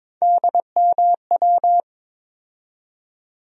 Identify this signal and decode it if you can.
Morse code, heard as a keyed tone, about 22 wpm, 700 Hz. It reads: DMW
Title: USA Call Sign Postfixes - 3 Letters 22wpm